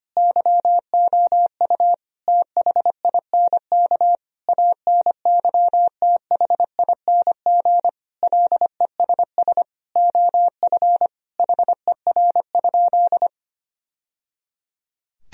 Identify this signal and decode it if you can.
Morse code, heard as a keyed tone, about 25 wpm, 700 Hz. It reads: YOU THINK ANYTHING LESS OF HER?
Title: Sentences from Top 1000 Words - Review Part 2 - 25wpm